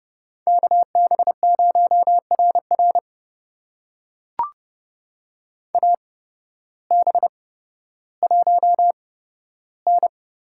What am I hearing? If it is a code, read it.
Morse code, heard as a keyed tone, about 30 wpm, spaced as for 14 wpm, 700 Hz. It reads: KB0RR AB1N
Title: USA Call Signs - 5 Characters - 8x Character Spacing - 30wpm